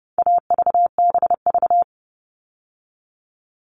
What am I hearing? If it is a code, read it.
Morse code, heard as a keyed tone, about 30 wpm, 700 Hz. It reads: AVBV
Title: All Cut Numbers - 4-Digit 30wpm